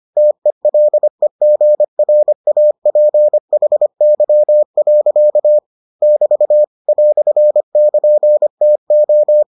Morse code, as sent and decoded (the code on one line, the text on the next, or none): - . .-.. --. .-. .- .--. .... -.-- .-.-.- / -...- / .-..-. -.--. - ---
TELEGRAPHY. <BT> "<KN>TO